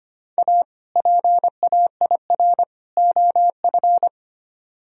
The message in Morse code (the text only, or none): A PAIR OF